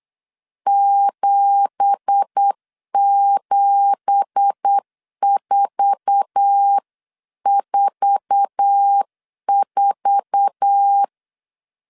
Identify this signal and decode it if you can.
Morse code, heard as a keyed tone, about 8 wpm, 800 Hz. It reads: 77444
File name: CW / DET / 1 9 1500C S C 8wpm 5653